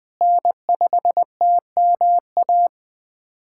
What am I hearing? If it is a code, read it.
Morse code, heard as a keyed tone, about 20 wpm, 700 Hz. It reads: N5TMA
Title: USA Callsign DE Callsign - 5 Characters - 20wpm